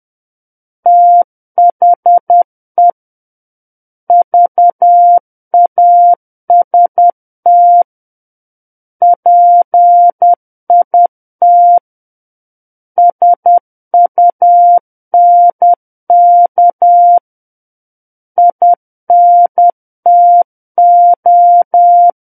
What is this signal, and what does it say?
Morse code, heard as a keyed tone, about 10 wpm, 700 Hz. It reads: THE VAST PIT SUNK INTO
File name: Book One / 01-Chapter 1 - 10 WPM